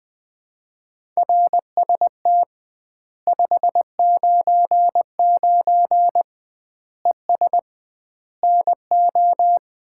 Morse code, heard as a keyed tone, about 20 wpm, 700 Hz. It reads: RST 599 ES NO
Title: Common QSO Phrases - 2x Word-Spacing - 20wpm